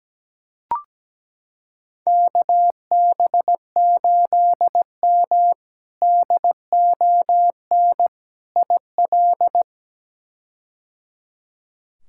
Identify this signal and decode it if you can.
Morse code, heard as a keyed tone, about 17 wpm, 700 Hz. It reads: KB8M DON IL
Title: K1USN SST QSO Practice - 17wpm